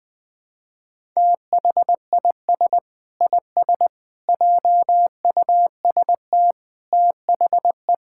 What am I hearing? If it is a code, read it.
Morse code, heard as a keyed tone, about 20 wpm, 700 Hz. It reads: THIS IS JUST THE
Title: Sentences with 6-Word Phrases - Spoken Beginning - 20wpm